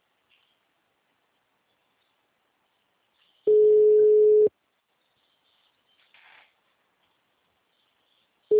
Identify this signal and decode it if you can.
Ringback tone (Europe, 425 Hz).
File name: out- -2014- - - .621